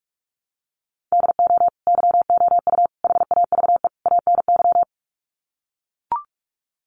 Morse code, heard as a keyed tone, about 44 wpm, 700 Hz. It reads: DO YOU HAVE ANY
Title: Sets of 4 Words 45wpm